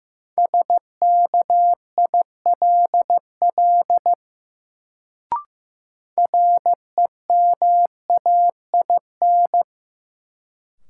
Morse code, heard as a keyed tone, about 15 wpm, 700 Hz. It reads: SKILL REMAIN
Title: Sentences from Top 1200 Words - New Word Drill - 15wpm